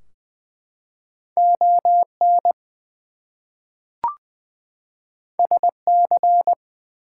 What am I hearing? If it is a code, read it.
Morse code, heard as a keyed tone, about 20 wpm, 700 Hz. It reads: ON SC
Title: US States and Canadian Provinces 20wpm